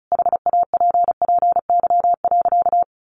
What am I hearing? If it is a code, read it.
Morse code, heard as a keyed tone, about 34 wpm, 700 Hz. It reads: HAPPY.
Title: Sentences from Top 1000 Words 35wpm